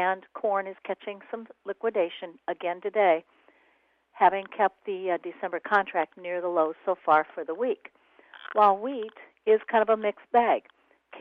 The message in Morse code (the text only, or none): none